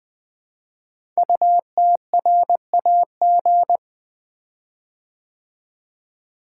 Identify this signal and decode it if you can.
Morse code, heard as a keyed tone, about 20 wpm, 700 Hz. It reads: UTRAG
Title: Code Groups - 5 Letter:Numbers - 20wpm